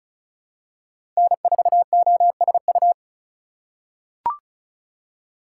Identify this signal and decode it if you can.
Morse code, heard as a keyed tone, about 34 wpm, 700 Hz. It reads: N4OSU
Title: USA Call Signs - 5 Characters - 4x Character Spacing - 35wpm